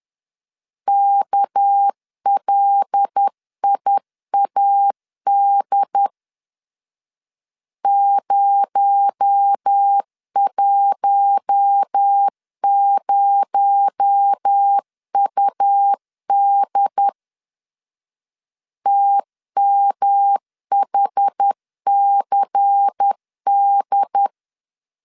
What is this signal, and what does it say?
Morse code, heard as a keyed tone, about 10 wpm, 800 Hz. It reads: KLIAD 010UD TMHCD
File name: LEZIONI / 6 PKX0 / 6 PKX0 10wpm 2707